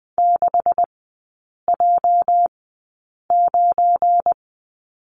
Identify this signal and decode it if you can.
Morse code, heard as a keyed tone, about 20 wpm, spaced as for 8 wpm, 700 Hz. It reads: 6J9
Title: 205 Lesson 33e - Letter j - Three Characters 20wpm